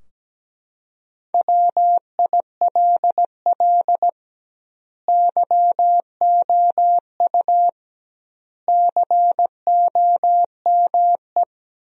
Morse code, heard as a keyed tone, about 17 wpm, 700 Hz. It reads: WILL YOU COME
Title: 2x Word-Spacing - Sentences from Top 200 Words - 17wpm